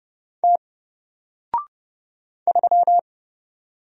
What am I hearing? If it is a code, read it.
Morse code, heard as a keyed tone, about 30 wpm, 700 Hz. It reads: T 3
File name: Single-Letter-Number-30wpm